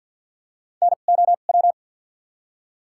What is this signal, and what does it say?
Morse code, heard as a keyed tone, about 50 wpm, spaced as for 36 wpm, 700 Hz. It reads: NOW
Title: 2x Character Spacing - Top 300 Words - 50wpm